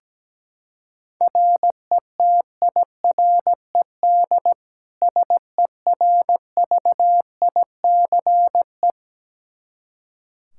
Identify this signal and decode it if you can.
Morse code, heard as a keyed tone, about 17 wpm, 700 Hz. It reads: RETIRED SERVICE
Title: K1USN SST QSO Practice 17wpm